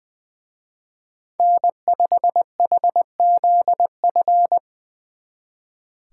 Morse code, heard as a keyed tone, about 21 wpm, 700 Hz. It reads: N5HZF
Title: Speed Racing - USA 1x3 Call Signs 20wpm